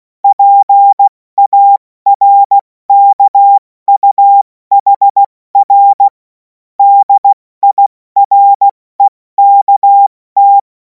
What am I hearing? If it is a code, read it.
Morse code, heard as a keyed tone, about 16 wpm, 800 Hz. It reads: PARKUHR DIREKT
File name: Morsekurs / Witze / Witz 11. August 2026 70 F80